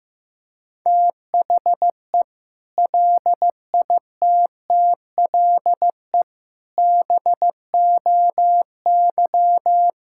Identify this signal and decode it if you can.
Morse code, heard as a keyed tone, about 15 wpm, 700 Hz. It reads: THE LITTLE BOY
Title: Sentences from Top 1600 Words 15wpm